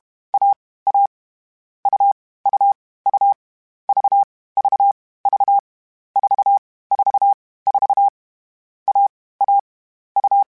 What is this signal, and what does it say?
Morse code, heard as a keyed tone, about 32 wpm, 800 Hz. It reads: A A U U U V V V 4 4 4 A A U
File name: CW Code Groups-mp3s / Sets mp3s / AUV AUV4 sp @30